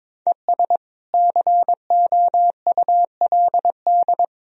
Morse code, heard as a keyed tone, about 22 wpm, 700 Hz. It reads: ES COULD
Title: Sentences from Top 100 Words and Abbreviations 22wpm